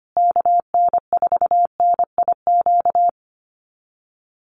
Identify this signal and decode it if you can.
Morse code, heard as a keyed tone, about 25 wpm, 700 Hz. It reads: KN4NIQ
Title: USA Call Signs Repeated 3x 25wpm